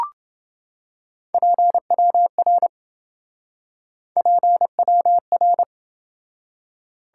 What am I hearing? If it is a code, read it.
Morse code, heard as a keyed tone, about 27 wpm, 700 Hz. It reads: PWR PWR